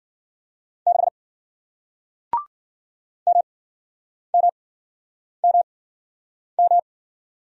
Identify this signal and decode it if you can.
Morse code, heard as a keyed tone, about 53 wpm, spaced as for 9 wpm, 700 Hz. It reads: 6 MMMM